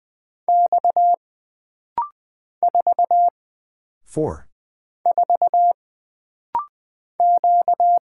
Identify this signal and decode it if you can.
Morse code, heard as a keyed tone, about 20 wpm, 700 Hz. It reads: X 4 4 Q
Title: Single Character (Letter:Number:Punctuation:Prosign) - Rapid-Fire - 20wpm